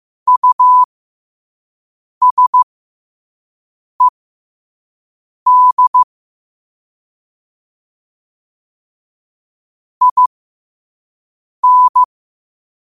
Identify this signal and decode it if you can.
Morse code, heard as a keyed tone, about 15 wpm, spaced as for 5 wpm, 1000 Hz. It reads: USED IN
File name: qotdfiles / 5wpm